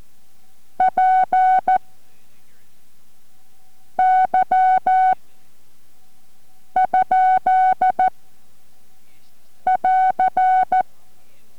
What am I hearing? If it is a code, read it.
Morse code, heard as a keyed tone, about 14 wpm, spaced as for 4 wpm, 750 Hz. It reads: PY? <AR>